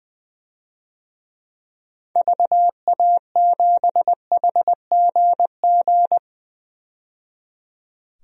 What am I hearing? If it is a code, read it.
Morse code, heard as a keyed tone, about 20 wpm, 700 Hz. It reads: VA7HGG
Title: DXCC-100 Call Signs - 6 Characters - 20wpm